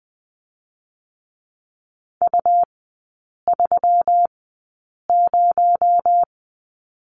Morse code, heard as a keyed tone, about 20 wpm, spaced as for 8 wpm, 700 Hz. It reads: U30
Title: 218 Lesson 35e - Number 0 - Three Characters 20wpm